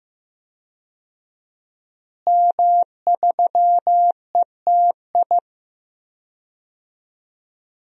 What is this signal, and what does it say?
Morse code, heard as a keyed tone, about 15 wpm, 700 Hz. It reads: M3ETI
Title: DXCC-100 Call Signs - 5 Characters - Repeated 3x - 15wpm